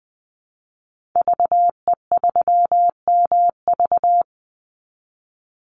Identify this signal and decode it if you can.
Morse code, heard as a keyed tone, about 20 wpm, 700 Hz. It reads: VE3MV
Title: Call Signs Repeated 3x - 20wpm